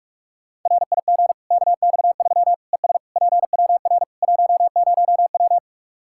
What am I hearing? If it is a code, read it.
Morse code, heard as a keyed tone, about 46 wpm, 700 Hz. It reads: RIG KX3 ES PWR 10W
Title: Practice QSO - Exchange 2 - Info - 45wpm